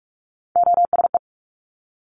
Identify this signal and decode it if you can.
Morse code, heard as a keyed tone, about 46 wpm, 700 Hz. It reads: OSE
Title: Top Three Letter Combinations in Words 45wpm